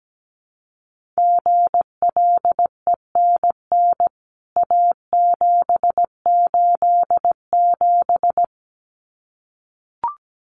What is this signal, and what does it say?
Morse code, heard as a keyed tone, about 17 wpm, 700 Hz. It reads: GLENN A787
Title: CWT Exchange 17wpm